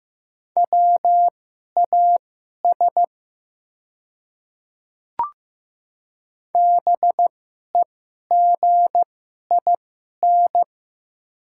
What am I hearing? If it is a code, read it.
Morse code, heard as a keyed tone, about 15 wpm, spaced as for 11 wpm, 700 Hz. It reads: WAS BEGIN
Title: 2x Character Spacing - Top 300 Words - 15wpm